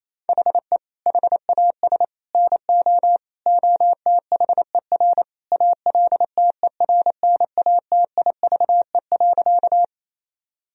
Morse code, heard as a keyed tone, about 27 wpm, 700 Hz. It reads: HE HAS NO OTHER ALTERNATIVE.
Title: Sentences from Top 1000 Words - Review Part 4 - 28wpm